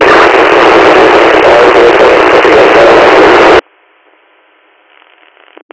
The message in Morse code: none

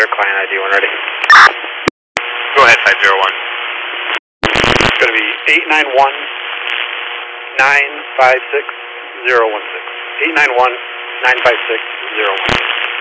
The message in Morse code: none

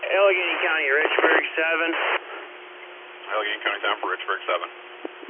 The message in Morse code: none

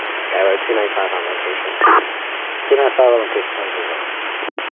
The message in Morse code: none

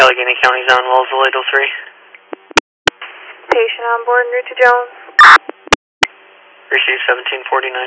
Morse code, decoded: none

